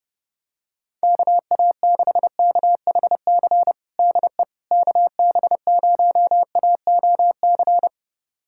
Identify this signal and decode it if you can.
Morse code, heard as a keyed tone, about 30 wpm, 700 Hz. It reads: KA6KHC DE KB0AOC
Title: USA Callsign DE Callsign - 6 Characters - 30wpm